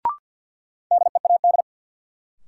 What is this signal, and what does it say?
Morse code, heard as a keyed tone, about 52 wpm, 700 Hz. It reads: DEAD
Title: Sentences from Top 1500 Words - New Word Drill Alt - 50wpm